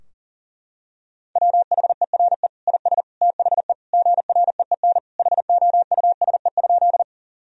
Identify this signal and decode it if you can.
Morse code, heard as a keyed tone, about 40 wpm, 700 Hz. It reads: WHERE IS THE GREEN HOUSE?
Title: Sentences from Top 1000 Words - Review Part 2 - 40wpm